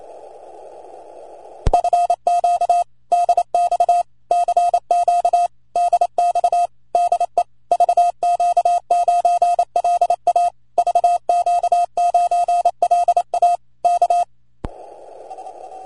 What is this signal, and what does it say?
Morse code, heard as a keyed tone, about 27 wpm, 700 Hz. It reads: FQ DX CQ DX DE VQ9LA VQ9LA K